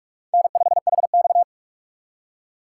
Morse code, heard as a keyed tone, about 59 wpm, spaced as for 51 wpm, 700 Hz. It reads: N5HX
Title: Speed Racing - USA Call Signs 45wpm